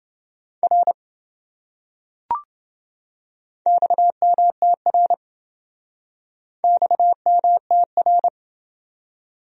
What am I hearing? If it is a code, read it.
Morse code, heard as a keyed tone, about 28 wpm, 700 Hz. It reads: R XMTR XMTR